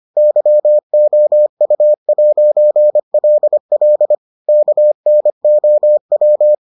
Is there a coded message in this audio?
Yes: Morse code from 0.2 to 6.5 s, heard as a keyed tone, about 25 wpm, 600 Hz. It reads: YOU'LL KNOW